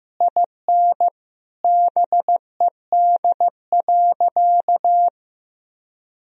Morse code, heard as a keyed tone, about 15 wpm, 700 Hz. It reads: IN BED.
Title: Sentences from Top 1000 Words - Review Part 1 - 15wpm